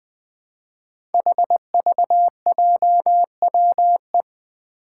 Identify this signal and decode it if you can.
Morse code, heard as a keyed tone, about 20 wpm, 700 Hz. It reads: HVJWE
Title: Five Challenging Letter-Number Combinations 20wpm